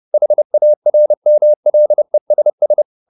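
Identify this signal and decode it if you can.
Morse code, heard as a keyed tone, about 30 wpm, 600 Hz. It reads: HARMLESS
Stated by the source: Morse code at 30 wpm, the novel Never by Ken Follett in English, from CW med Gnister.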